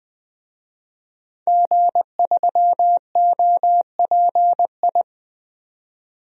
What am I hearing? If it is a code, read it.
Morse code, heard as a keyed tone, about 20 wpm, 700 Hz. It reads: G3OPI
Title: DXCC-100 Call Signs - 5 Characters - 20wpm